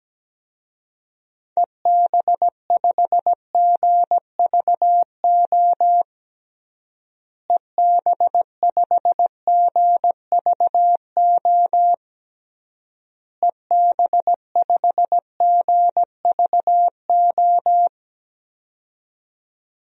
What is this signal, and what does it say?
Morse code, heard as a keyed tone, about 17 wpm, 700 Hz. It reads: EB5GVO EB5GVO EB5GVO
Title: DXCC-100 Call Signs - 6 Characters - Repeated 3x - 17wpm